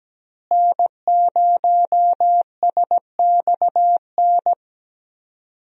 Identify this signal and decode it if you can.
Morse code, heard as a keyed tone, about 17 wpm, 700 Hz. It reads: N0SXN
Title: USA Call Signs - 5 Characters - Repeated 2x - 17wpm